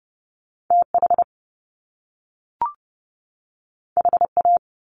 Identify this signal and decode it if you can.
Morse code, heard as a keyed tone, about 30 wpm, 700 Hz. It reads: TH HA
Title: Top Two Letter Combinations in Words 30wpm